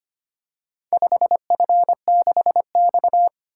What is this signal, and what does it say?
Morse code, heard as a keyed tone, about 25 wpm, 700 Hz. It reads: 5F6X
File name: practice / 25wpm / Four Letter-Number Combinations - 25wpm